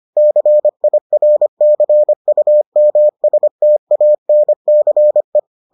Morse code, heard as a keyed tone, about 25 wpm, 600 Hz. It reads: CIRCUMSTANCE